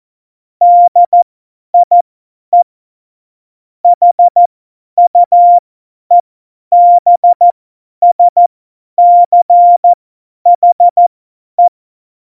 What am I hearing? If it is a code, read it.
Morse code, heard as a keyed tone, about 14 wpm, spaced as for 10 wpm, 700 Hz. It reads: DIE HUEBSCHE
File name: Morsekurs / Witze / Witz 11 Juli 17, Tempo 50 F70